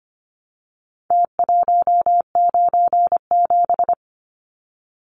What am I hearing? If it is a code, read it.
Morse code, heard as a keyed tone, about 25 wpm, 700 Hz. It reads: T197
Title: Numbers - Contest Style - 4-Digit 25wpm